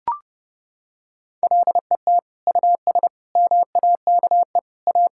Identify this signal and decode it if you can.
Morse code, heard as a keyed tone, about 30 wpm, 700 Hz. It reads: LET US MAKE A
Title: Sentences from Top 1600 Words 30wpm